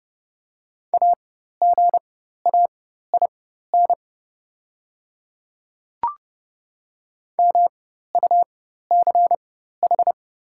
Morse code, heard as a keyed tone, about 30 wpm, spaced as for 14 wpm, 700 Hz. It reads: AGAIN MUCH
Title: 4x Character Spacing - Top 200 Words - 30wpm